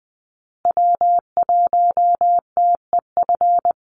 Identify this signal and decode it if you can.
Morse code, heard as a keyed tone, about 20 wpm, 700 Hz. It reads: W1TEF
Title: Call Signs Repeated 2x - 20wpm